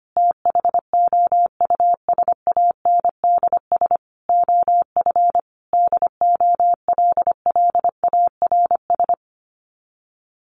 Morse code, heard as a keyed tone, about 25 wpm, 700 Hz. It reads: THOUSANDS OF DOLLARS
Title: Sets of 5 Words Encore2 25wpm